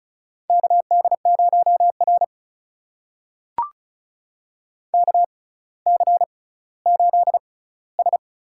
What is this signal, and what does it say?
Morse code, heard as a keyed tone, about 36 wpm, spaced as for 18 wpm, 700 Hz. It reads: KD0R KC8S
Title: USA Call Signs - 6x Character Spacing - 35wpm